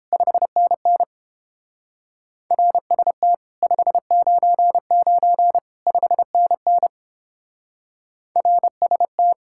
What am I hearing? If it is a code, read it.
Morse code, heard as a keyed tone, about 29 wpm, 700 Hz. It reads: 5NN RST 599 5NN RST